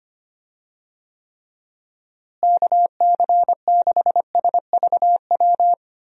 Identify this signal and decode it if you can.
Morse code, heard as a keyed tone, about 25 wpm, 700 Hz. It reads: KC6SVW
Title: USA Call Signs - 6 Characters - Repeated 2x - 25wpm